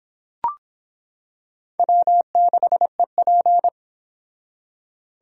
Morse code, heard as a keyed tone, about 26 wpm, 700 Hz. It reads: W6EP